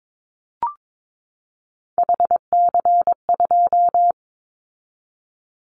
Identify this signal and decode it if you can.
Morse code, heard as a keyed tone, about 22 wpm, 700 Hz. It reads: HC2
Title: Callsign Prefixes for Top 100 DXCC - 22wpm